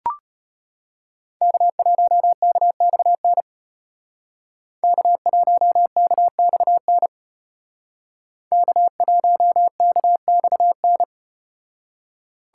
Morse code, heard as a keyed tone, about 34 wpm, 700 Hz. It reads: K1KXN K1KXN K1KXN